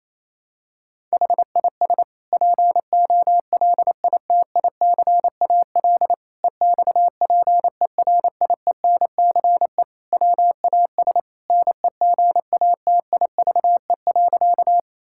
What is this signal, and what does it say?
Morse code, heard as a keyed tone, about 28 wpm, 700 Hz. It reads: HIS POLITICAL EXPERIENCE WAS NEGATIVE.
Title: Sentences from Top 1000 Words - Review Part 6 - 28wpm